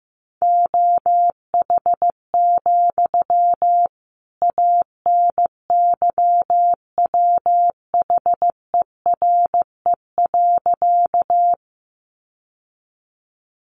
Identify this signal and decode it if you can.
Morse code, heard as a keyed tone, about 15 wpm, 700 Hz. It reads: OH, ANYWHERE.
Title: Sentences from Top 1400 Words 15wpm